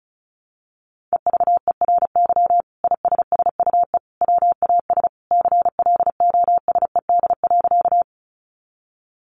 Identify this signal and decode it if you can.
Morse code, heard as a keyed tone, about 34 wpm, 700 Hz. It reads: EVERY ISSUE WAS CLOSED.
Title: Sentences from Top 1400 Words 35wpm